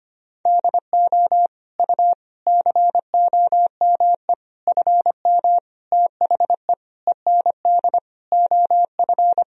Morse code, heard as a keyed tone, about 25 wpm, 700 Hz. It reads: DO U COME FM THE END OF
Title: Sentences from Top 200 Words and Abbreviations 25wpm